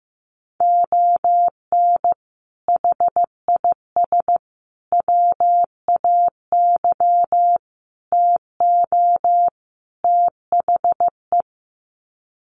Morse code, heard as a keyed tone, about 15 wpm, 700 Hz. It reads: ON HIS WAY TO THE
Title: Sets of 5 Words Encore2 15wpm